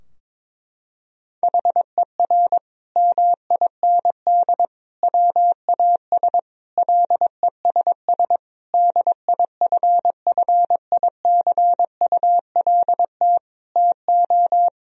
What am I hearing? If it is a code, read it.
Morse code, heard as a keyed tone, about 21 wpm, 700 Hz. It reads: HER MIND WAS LESS DIFFICULT TO
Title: Sentences from Top 1000 Words - Review Part 2 - 22wpm